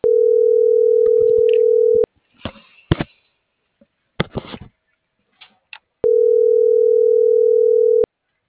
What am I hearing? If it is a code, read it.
Ringback tone (North America), 440 + 480 Hz.